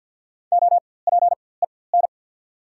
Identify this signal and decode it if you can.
Morse code, heard as a keyed tone, about 54 wpm, spaced as for 24 wpm, 690 Hz. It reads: OPEN
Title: 4x Character Spacing - Top 300 Words - 50wpm